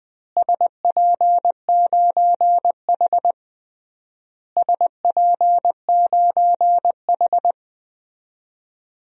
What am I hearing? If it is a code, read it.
Morse code, heard as a keyed tone, about 20 wpm, 700 Hz. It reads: SP9H SP9H
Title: DXCC-100 Call Signs - 4 Characters - Repeated 3x - 20wpm